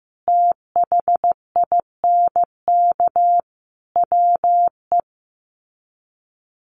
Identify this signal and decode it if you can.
Morse code, heard as a keyed tone, about 15 wpm, 700 Hz. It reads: THINK WE
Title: Sets of 5 Words 15wpm